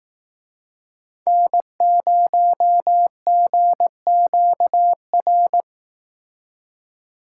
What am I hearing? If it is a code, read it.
Morse code, heard as a keyed tone, about 19 wpm, 700 Hz. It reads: N0GQR